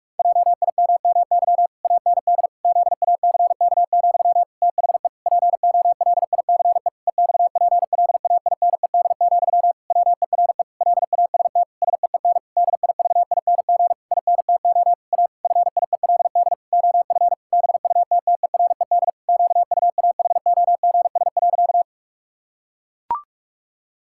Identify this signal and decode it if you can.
Morse code, heard as a keyed tone, about 46 wpm, 700 Hz. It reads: JIMMY AND ZACK, THE POLICE EXPLAINED, WERE LAST SEEN DIVING INTO A FIELD OF BUTTERED QUAHOGS.
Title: Pangram Challenge v2 - 45wpm